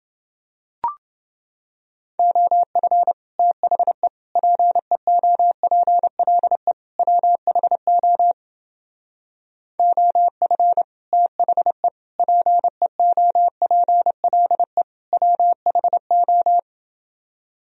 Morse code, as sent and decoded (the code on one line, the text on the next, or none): --- ..-. / - .... / .--. --- .--. .-.. / .-- .... --- / --- ..-. / - .... / .--. --- .--. .-.. / .-- .... ---
OF THE PEOPLE WHO OF THE PEOPLE WHO